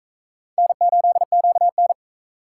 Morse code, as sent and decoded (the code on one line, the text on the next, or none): -. ---.. --.- -.
N8QN